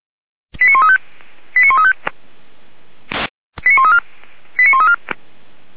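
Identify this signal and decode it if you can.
Five-tone selective call (ZVEI), heard as quick runs of five tones, tones of about 70 ms. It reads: 89135 89135 89134 89134
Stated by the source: ZVEI five-tone call 89136 received on an SDR.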